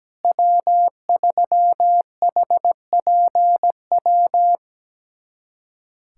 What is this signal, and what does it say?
Morse code, heard as a keyed tone, about 17 wpm, 700 Hz. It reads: W3HPW
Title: USA Callsign DE Callsign - 5 Characters - 17wpm